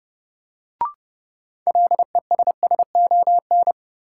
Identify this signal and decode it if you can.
Morse code, heard as a keyed tone, about 30 wpm, 700 Hz. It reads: LESSON
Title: Sentences from Top 1200 Words - New Word Drill - 30wpm